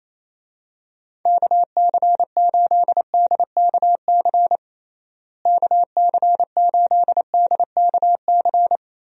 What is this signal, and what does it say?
Morse code, heard as a keyed tone, about 27 wpm, 700 Hz. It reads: KC8DKC KC8DKC
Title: USA Call Signs - 6 Characters - Repeated 3x - 28wpm